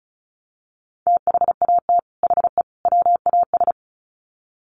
Morse code, heard as a keyed tone, about 34 wpm, 700 Hz. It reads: THAT HE WAS
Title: Sets of 3 Words 35wpm